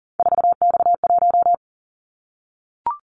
Morse code, heard as a keyed tone, about 40 wpm, 700 Hz. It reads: 4X1
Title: Callsign Prefixes for Top 50 DXCC - 40wpm